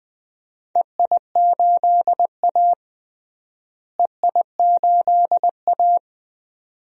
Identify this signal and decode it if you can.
Morse code, heard as a keyed tone, about 20 wpm, 700 Hz. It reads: EI8A EI8A
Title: DXCC-100 Call Signs - 4 Characters - Repeated 2x - 20wpm